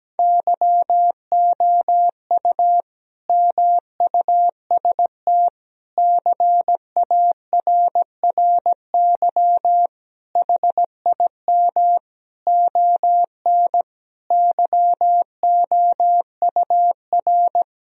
Morse code, heard as a keyed tone, about 17 wpm, 700 Hz. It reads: YOU MUST CARRY HIM ON YOUR
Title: Sentences from Top 1000 Words - Review Part 1 - 17wpm